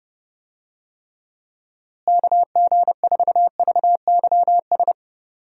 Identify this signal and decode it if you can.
Morse code, heard as a keyed tone, about 29 wpm, 700 Hz. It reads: KG4VYS
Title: USA Call Signs - 6 Characters - Repeated 2x - 30wpm